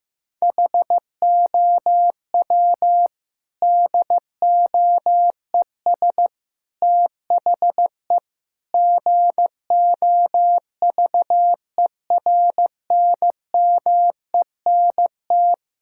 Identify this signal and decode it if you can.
Morse code, heard as a keyed tone, about 15 wpm, 700 Hz. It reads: HOW DOES THE GOVERNMENT
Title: Sentences from Top 1000 Words - Review Part 5 - 15wpm